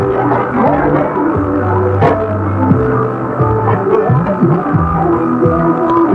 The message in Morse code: none